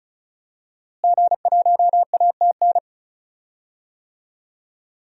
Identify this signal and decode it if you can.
Morse code, heard as a keyed tone, about 34 wpm, 700 Hz. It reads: G1ATN
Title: DXCC-100 Call Signs - 5 Characters - Repeated 3x - 35wpm